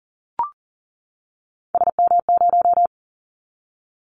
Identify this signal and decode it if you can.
Morse code, heard as a keyed tone, about 40 wpm, 700 Hz. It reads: SM0
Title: Callsign Prefixes for Top 25 DXCC - 40wpm